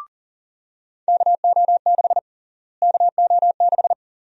Morse code, heard as a keyed tone, about 40 wpm, 700 Hz. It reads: KO6 KO6
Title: USA Call Sign Prefixes - Repeated 2x - 40wpm